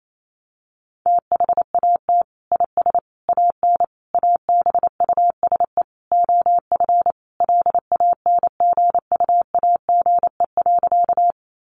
Morse code, heard as a keyed tone, about 28 wpm, 700 Hz. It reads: THAT IS AN ABUSE OF LANGUAGE.
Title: Sentences from Top 1300 Words 28wpm